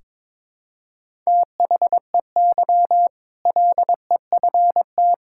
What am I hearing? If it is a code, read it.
Morse code, heard as a keyed tone, about 22 wpm, 700 Hz. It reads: THEY LEFT
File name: practice / 22wpm / Sentences with 2-Word Phrases - 22wpm